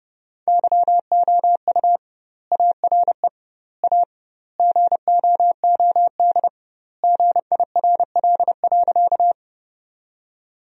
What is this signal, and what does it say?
Morse code, heard as a keyed tone, about 32 wpm, 700 Hz. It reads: YOU ARE A GOOD GIRL.
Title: 2x Word-Spacing - Sentences from Top 400 Words - 30wpm